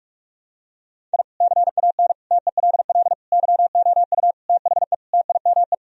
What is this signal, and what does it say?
Morse code, heard as a keyed tone, about 41 wpm, 700 Hz. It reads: I CAN TELL YOU THE TIME
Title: Sentences from Top 1000 Words - Review Part 1 - 45wpm